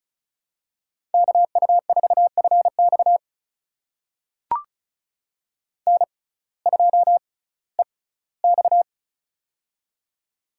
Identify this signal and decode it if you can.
Morse code, heard as a keyed tone, about 36 wpm, spaced as for 19 wpm, 700 Hz. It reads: KU4FX N2EX
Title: USA Call Signs - 6x Character Spacing - 35wpm